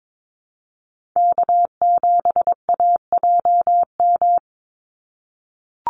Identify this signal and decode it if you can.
Morse code, heard as a keyed tone, about 22 wpm, 700 Hz. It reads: K7AJM
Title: Call Signs - 22wpm